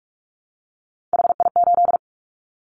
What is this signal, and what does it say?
Morse code, heard as a keyed tone, about 44 wpm, 700 Hz. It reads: HI8
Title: Callsign Prefixes for Top 100 DXCC - 45wpm